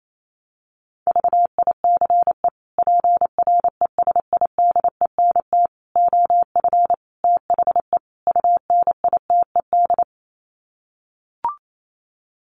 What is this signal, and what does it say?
Morse code, heard as a keyed tone, about 28 wpm, 700 Hz. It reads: VICE PRESIDENT OF THE UNITED
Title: Sets of 5 Words Encore2 28wpm